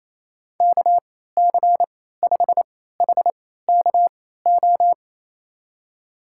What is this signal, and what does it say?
Morse code, heard as a keyed tone, about 28 wpm, spaced as for 16 wpm, 700 Hz. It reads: KC5HKO
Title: USA Call Signs - 6 Characters - 3x Character Spacing - 28wpm